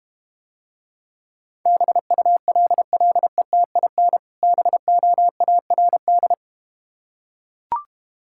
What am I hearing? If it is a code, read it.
Morse code, heard as a keyed tone, about 31 wpm, 700 Hz. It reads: BULLETIN BOARD
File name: Easy Words - Speed-Racing with Hints - 25wpm